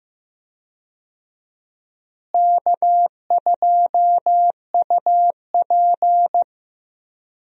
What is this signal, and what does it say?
Morse code, heard as a keyed tone, about 15 wpm, 700 Hz. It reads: K2UP
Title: USA Call Signs - 4 Characters - Repeated 3x - 15wpm